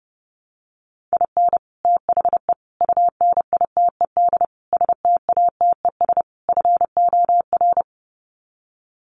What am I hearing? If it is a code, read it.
Morse code, heard as a keyed tone, about 30 wpm, 700 Hz. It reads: IN THE UNITED STATES FOR
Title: Sets of 5 Words Encore2 30wpm